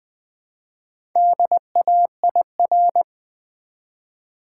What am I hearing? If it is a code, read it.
Morse code, heard as a keyed tone, about 20 wpm, 700 Hz. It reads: DAIR
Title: Build A Word - 1 Letter At A Time - 5-Letter Words - 20wpm